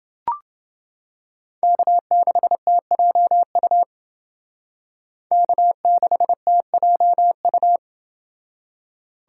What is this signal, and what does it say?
Morse code, heard as a keyed tone, about 28 wpm, 700 Hz. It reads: K6TJU K6TJU